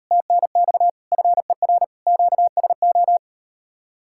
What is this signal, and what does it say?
Morse code, heard as a keyed tone, about 39 wpm, 700 Hz. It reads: TNX FER QSO